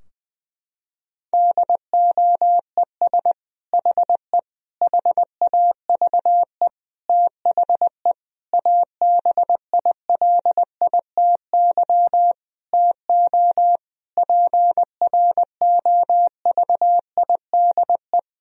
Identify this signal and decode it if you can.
Morse code, heard as a keyed tone, about 20 wpm, 700 Hz. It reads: DOES HE HAVE THE ABILITY TO PROVIDE
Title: Sentences from Top 1000 Words - Review Part 3 - 20wpm